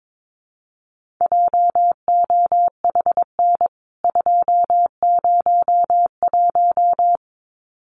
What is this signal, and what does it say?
Morse code, heard as a keyed tone, about 22 wpm, 700 Hz. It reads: JOHN 201
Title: CWT Exchange 22wpm